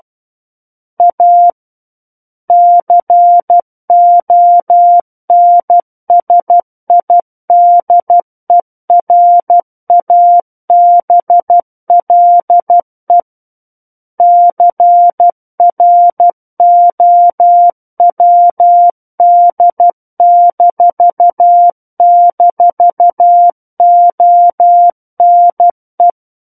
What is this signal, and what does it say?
Morse code, heard as a keyed tone, about 12 wpm, 700 Hz. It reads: A CONSIDERABLE CROWD--ONE
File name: Book One / 03-Chapter 3 - 12 WPM